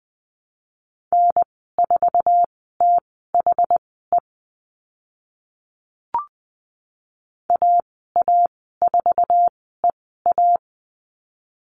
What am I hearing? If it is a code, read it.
Morse code, heard as a keyed tone, about 20 wpm, spaced as for 15 wpm, 700 Hz. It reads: N4THE AA4EA
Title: 062 Lesson 11g - Letter h - Call Signs 20wpm